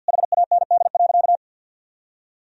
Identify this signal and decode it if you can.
Morse code, heard as a keyed tone, about 50 wpm, 700 Hz. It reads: HAND.
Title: Sentences from Top 1000 Words - Review Part 3 - 50wpm